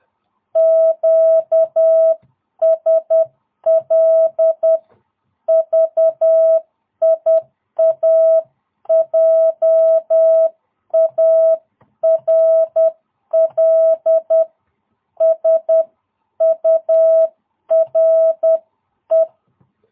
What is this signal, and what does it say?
Morse code, heard as a keyed tone, about 10 wpm, 660 Hz. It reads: QSL VIA JARL SURE
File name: jh8cbh / 2022cw